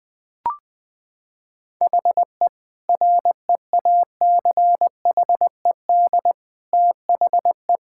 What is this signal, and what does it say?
Morse code, heard as a keyed tone, about 20 wpm, 700 Hz. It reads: HE REACHED THE
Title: Sentences with 3-Word Phrases - 20wpm